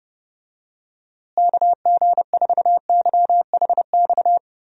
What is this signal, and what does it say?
Morse code, heard as a keyed tone, about 32 wpm, 700 Hz. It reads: KG4YHX
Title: USA Call Signs - 6 Characters - Repeated 3x - 30wpm